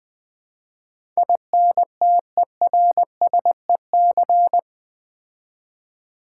Morse code, heard as a keyed tone, about 20 wpm, 700 Hz. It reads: INTERSEC
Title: Build A Word - 4 Letters At A Time - 12-Letter Words - 20wpm